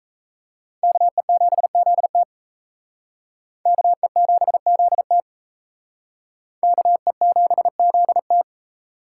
Morse code, heard as a keyed tone, about 35 wpm, 700 Hz. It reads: KE7ZT KE7ZT KE7ZT